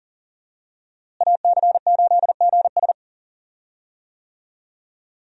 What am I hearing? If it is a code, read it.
Morse code, heard as a keyed tone, about 40 wpm, 700 Hz. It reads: AC8GS
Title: USA Call Signs - 5 Characters - Repeated 2x - 40wpm